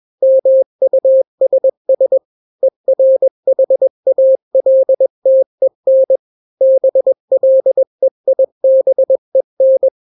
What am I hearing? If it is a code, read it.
Morse code, heard as a keyed tone, about 21 wpm, 530 Hz. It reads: MUSS ERHALTEN BLEIBEN
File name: daily / 20wpm